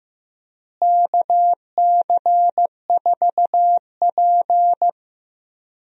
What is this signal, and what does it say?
Morse code, heard as a keyed tone, about 15 wpm, 700 Hz. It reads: KC4P
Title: USA Call Signs - 4 Characters - Repeated 2x - 15wpm